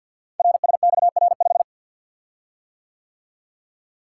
Morse code, heard as a keyed tone, about 50 wpm, 700 Hz. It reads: RSXR5
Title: Code Groups - 5 Letter:Numbers - 50wpm